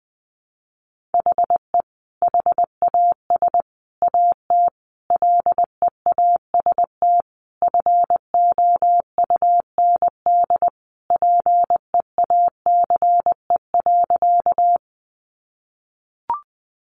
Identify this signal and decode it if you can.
Morse code, heard as a keyed tone, about 20 wpm, 700 Hz. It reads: HE HAS AT LEAST FOUND PEACE.
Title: Sentences from Top 1500 Words 20wpm